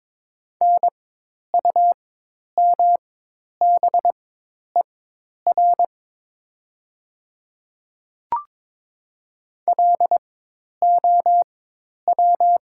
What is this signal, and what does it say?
Morse code, heard as a keyed tone, about 22 wpm, spaced as for 10 wpm, 700 Hz. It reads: NUMBER LOW
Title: 4x Character Spacing - Top 300 Words - 22wpm